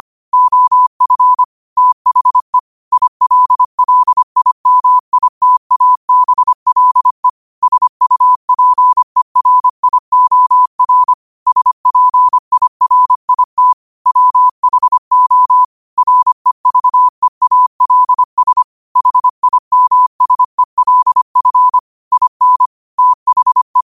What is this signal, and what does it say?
Morse code, heard as a keyed tone, about 25 wpm, 1000 Hz. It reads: OF THE ILLIMITABLE SUPERIOR SPIRIT WHO REVEALS HIMSELF IN THE
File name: qotdfiles / 25wpm